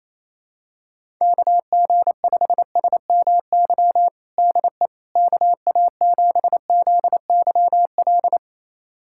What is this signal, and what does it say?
Morse code, heard as a keyed tone, about 28 wpm, 700 Hz. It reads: KG5SMY DE KA7ZYL
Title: USA Callsign DE Callsign - 6 Characters - 28wpm